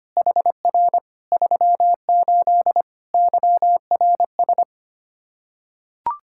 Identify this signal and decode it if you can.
Morse code, heard as a keyed tone, about 25 wpm, 700 Hz. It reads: HR 38 YRS
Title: Practice QSO - Exchange 3 - More Info - 25wpm